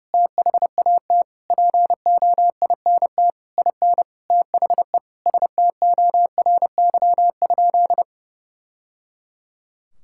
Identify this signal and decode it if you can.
Morse code, heard as a keyed tone, about 30 wpm, 700 Hz. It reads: THAT POINT IN THE STORY?
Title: Sentences from Top 1000 Words - Review Part 5 - 30wpm